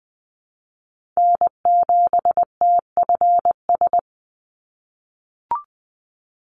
Morse code, heard as a keyed tone, about 20 wpm, 700 Hz. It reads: N7TFS
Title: USA Call Signs 20wpm